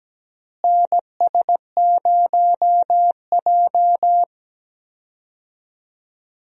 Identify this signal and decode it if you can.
Morse code, heard as a keyed tone, about 17 wpm, 700 Hz. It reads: NS0J
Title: USA Call Signs - 4 Characters - Repeated 2x - 17wpm